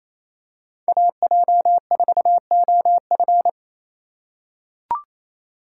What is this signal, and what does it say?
Morse code, heard as a keyed tone, about 28 wpm, 700 Hz. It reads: AJ4OF
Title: USA Call Signs - 5 Characters - 2x Character Spacing - 28wpm